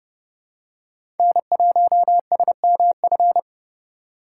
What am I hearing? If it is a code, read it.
Morse code, heard as a keyed tone, about 30 wpm, 700 Hz. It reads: N1SMF
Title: USA Call Signs - 4x Character Spacing - 30wpm